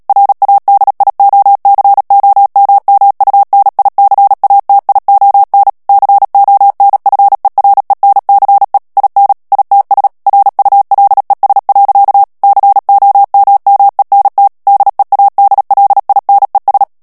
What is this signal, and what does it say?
Morse code, heard as a keyed tone, about 37 wpm, 800 Hz. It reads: RADIOCOMMUNICATION CONFERENCE IN ITS RULES. COMMENT DEADLINES
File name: morse mp3 / ANGLAIS texte clair 3 vitesse 35 mots minute